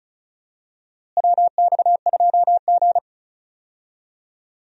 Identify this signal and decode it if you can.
Morse code, heard as a keyed tone, about 36 wpm, 700 Hz. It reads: WX2G
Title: USA Call Signs - 4 Characters - Repeated 2x - 35wpm